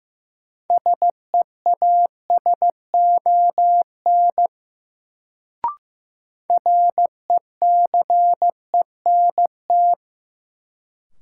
Morse code, heard as a keyed tone, about 15 wpm, 700 Hz. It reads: SEASON RECENT
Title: Sentences from Top 1400 Words - New Word Drill Alt - 15wpm